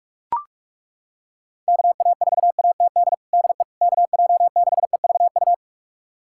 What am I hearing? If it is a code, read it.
Morse code, heard as a keyed tone, about 41 wpm, 700 Hz. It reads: KA4ATD DE KJ6EVU
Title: USA Callsign DE Callsign - 6 Characters - 45wpm